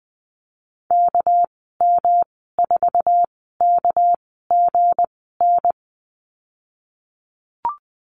Morse code, heard as a keyed tone, about 20 wpm, spaced as for 14 wpm, 700 Hz. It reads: KM4KGN
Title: 207 Lesson 33g - Letter j - Call Signs 20wpm